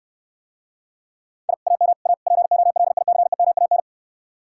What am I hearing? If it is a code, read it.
Morse code, heard as a keyed tone, about 69 wpm, 700 Hz. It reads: I AM A POLICEMAN